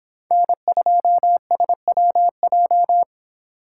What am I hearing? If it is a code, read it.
Morse code, heard as a keyed tone, about 29 wpm, 700 Hz. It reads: N2SWJ